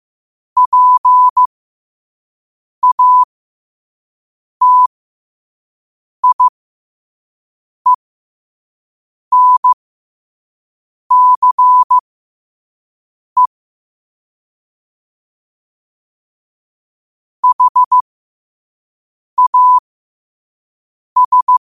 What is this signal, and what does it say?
Morse code, heard as a keyed tone, about 15 wpm, spaced as for 5 wpm, 1000 Hz. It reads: PATIENCE HAS
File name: qotdfiles / 5wpm